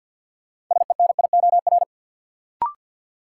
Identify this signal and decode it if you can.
Morse code, heard as a keyed tone, about 52 wpm, 700 Hz. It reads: SENIOR
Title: Sets of 5 Words Encore - New Word Drill - 50wpm